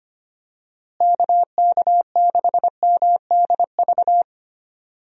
Morse code, heard as a keyed tone, about 25 wpm, 700 Hz. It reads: KK6MDV
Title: USA Call Signs - 6 Characters - Repeated 3x - 25wpm